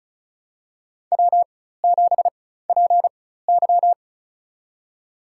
Morse code, heard as a keyed tone, about 36 wpm, spaced as for 16 wpm, 700 Hz. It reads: W7PY
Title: USA Call Signs - 4 Characters - 4x Character Spacing - 35wpm